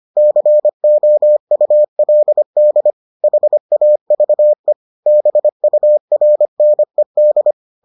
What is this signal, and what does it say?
Morse code, heard as a keyed tone, about 25 wpm, 600 Hz. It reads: COULD HAVE BURNED